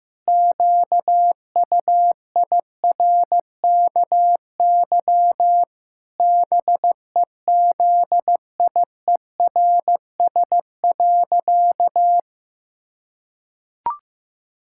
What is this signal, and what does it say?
Morse code, heard as a keyed tone, about 15 wpm, 700 Hz. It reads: QUIRKY BEZIERS.
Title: Pangram Challenge v2 - 15wpm